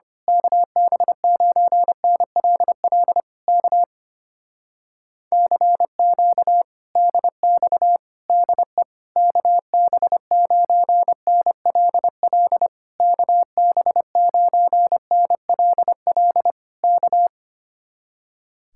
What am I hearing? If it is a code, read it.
Morse code, heard as a keyed tone, about 24 wpm, 700 Hz. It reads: KB9NLL K CQ DX DE KB9NLL KB9NLL K
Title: Speed-Racing Common QSO Phrases - 25wpm